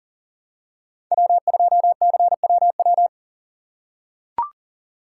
Morse code, heard as a keyed tone, about 40 wpm, 700 Hz. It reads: W2CWW
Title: USA Call Signs - 5 Characters - 2x Character Spacing - 40wpm